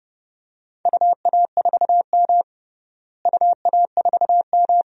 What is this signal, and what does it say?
Morse code, heard as a keyed tone, about 30 wpm, 700 Hz. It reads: UA4M UA4M
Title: DXCC-100 Call Signs - 4 Characters - Repeated 2x - 30wpm